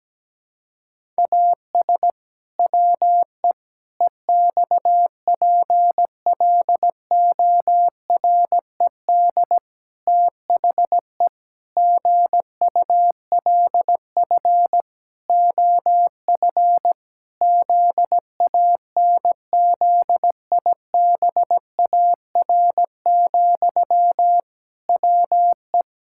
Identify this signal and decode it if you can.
Morse code, heard as a keyed tone, about 17 wpm, 700 Hz. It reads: AS WE EXPLORED THE GULF OF ZANZIBAR, WE
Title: Pangram Challenge v2 - 17wpm